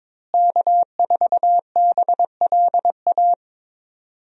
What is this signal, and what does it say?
Morse code, heard as a keyed tone, about 22 wpm, 700 Hz. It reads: K4BLA
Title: USA Call Signs - 5 Characters - Repeated 3x - 22wpm